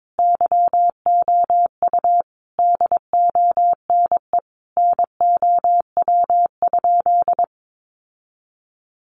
Morse code, heard as a keyed tone, about 22 wpm, 700 Hz. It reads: YOU DONE NOW?
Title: Sentences from Top 1100 Words 22wpm